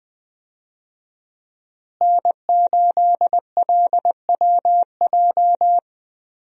Morse code, heard as a keyed tone, about 20 wpm, 700 Hz. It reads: N8LWJ
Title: USA Call Signs - 5 Characters - Repeated 3x - 20wpm